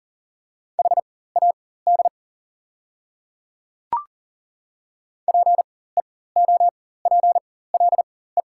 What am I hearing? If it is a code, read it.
Morse code, heard as a keyed tone, about 40 wpm, spaced as for 19 wpm, 700 Hz. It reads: HAD PEOPLE
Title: 4x Character Spacing - Top 100 Words - 40wpm